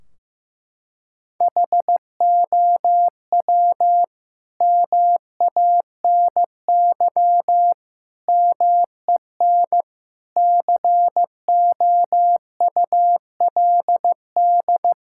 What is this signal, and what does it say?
Morse code, heard as a keyed tone, about 15 wpm, 700 Hz. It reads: HOW MANY MEN COULD
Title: Sentences from Top 1600 Words 15wpm